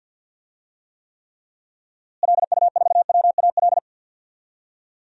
Morse code, heard as a keyed tone, about 49 wpm, 700 Hz. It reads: LU4WAL
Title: DXCC-100 Call Signs - 6 Characters - Repeated 3x - 50wpm